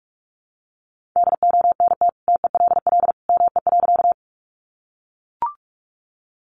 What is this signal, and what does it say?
Morse code, heard as a keyed tone, about 44 wpm, 700 Hz. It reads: DONT TELL ME.
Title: Sentences from Top 300 Words 45wpm